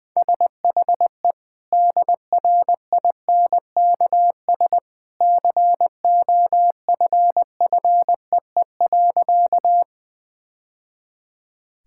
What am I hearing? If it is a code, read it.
Morse code, heard as a keyed tone, about 20 wpm, 700 Hz. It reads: SHE DRINKS COFFEE.
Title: Easy Sentences with Hints - 20wpm